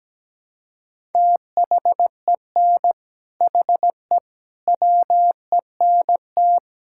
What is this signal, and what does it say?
Morse code, heard as a keyed tone, about 17 wpm, 700 Hz. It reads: THEN HE WENT
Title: Sentences from Top 1000 Words - Review Part 5 - 17wpm